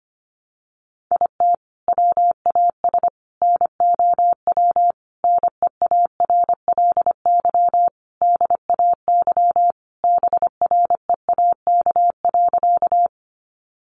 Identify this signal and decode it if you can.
Morse code, heard as a keyed tone, about 25 wpm, 700 Hz. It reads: IT WAS NOW NEARLY DAY BREAK.
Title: Sentences from Top 1300 Words 25wpm